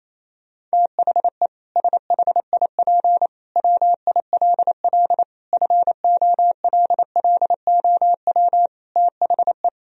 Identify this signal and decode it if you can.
Morse code, heard as a keyed tone, about 28 wpm, 700 Hz. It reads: THE SHIP WILL FOLLOW THE
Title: Sentences from Top 1000 Words - Review Part 4 - 28wpm